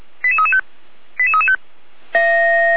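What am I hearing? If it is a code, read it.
Five-tone selective call (ZVEI), heard as quick runs of five tones, tones of about 70 ms. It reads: 88335 88335
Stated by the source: Five-tone ZVEI selective call received by SDR.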